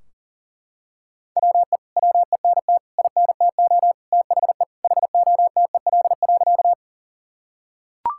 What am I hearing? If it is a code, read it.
Morse code, heard as a keyed tone, about 40 wpm, 700 Hz. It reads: WE WENT INTO THE HOTEL.
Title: Sentences from Top 1600 Words 40wpm